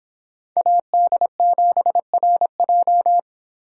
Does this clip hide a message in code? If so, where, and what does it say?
Yes: Morse code from 0.6 to 3.2 s, heard as a keyed tone, about 30 wpm, 700 Hz. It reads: AD7RJ